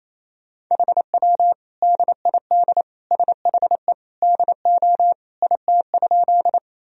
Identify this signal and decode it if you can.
Morse code, heard as a keyed tone, about 28 wpm, 700 Hz. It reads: HW DID SHE DO IT?
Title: Sentences from Top 100 Words and Abbreviations 28wpm